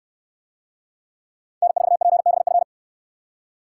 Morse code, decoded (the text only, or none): N4PBF